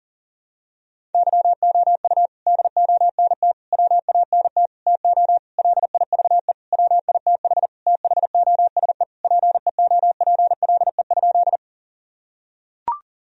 Morse code, heard as a keyed tone, about 40 wpm, 700 Hz. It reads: YOU DONT WANT TO LIVE WITH THOSE PEOPLE?
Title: Sentences from Top 1000 Words - Review Part 1 - 40wpm